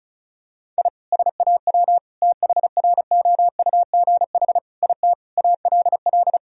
Morse code, heard as a keyed tone, about 36 wpm, 700 Hz. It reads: I SAW THROUGH IT ALL
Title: Sentences from Top 1000 Words - Review Part 1 - 35wpm